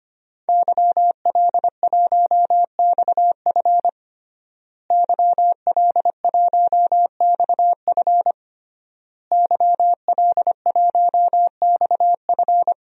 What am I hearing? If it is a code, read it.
Morse code, heard as a keyed tone, about 25 wpm, 700 Hz. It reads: YL1XF YL1XF YL1XF
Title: DXCC-100 Call Signs - 5 Characters - Repeated 3x - 25wpm